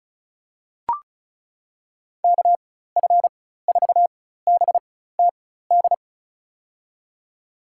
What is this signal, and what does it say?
Morse code, heard as a keyed tone, about 34 wpm, spaced as for 16 wpm, 700 Hz. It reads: KF4BTD
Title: USA Call Signs - 6 Characters - 4x Character Spacing - 35wpm